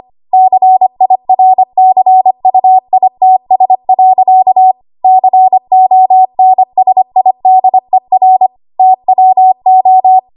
Morse code, heard as a keyed tone, about 25 wpm, 750 Hz. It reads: CIRCUITS. CONSIDER TWO
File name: Archive / 25 WPM / 25WPM